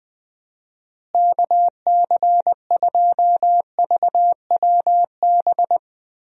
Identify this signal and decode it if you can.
Morse code, heard as a keyed tone, about 20 wpm, 700 Hz. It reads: KC2VWB
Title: USA Call Signs - 6 Characters - Repeated 3x - 20wpm